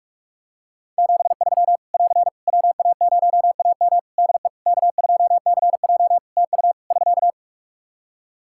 Morse code, heard as a keyed tone, about 41 wpm, 700 Hz. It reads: 73 <AR> WA0AM DE K2CJ TU <SK>
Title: K1USN SST QSO Practice 45wpm